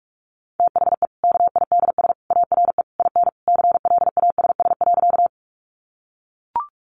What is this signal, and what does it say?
Morse code, heard as a keyed tone, about 44 wpm, 700 Hz. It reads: THE KIDS ARE IN CLASS.
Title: Easy Phrases 45wpm